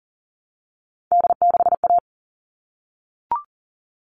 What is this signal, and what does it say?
Morse code, heard as a keyed tone, about 40 wpm, 700 Hz. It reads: D6A
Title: All Cut Numbers - 3-Digit 40wpm